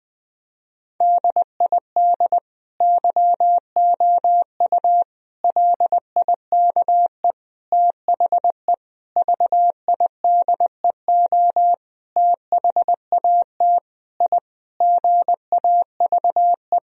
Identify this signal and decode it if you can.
Morse code, heard as a keyed tone, about 21 wpm, 700 Hz. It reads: DID YOU LIKE THE VIDEO THAT I GAVE
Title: Sentences from Top 1000 Words - Review Part 4 - 20wpm